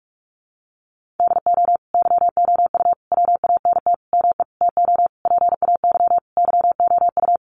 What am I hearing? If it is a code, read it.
Morse code, heard as a keyed tone, about 44 wpm, 700 Hz. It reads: DO YOU WANT ME TO PAY YOU
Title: Sentences from Top 1400 Words 45wpm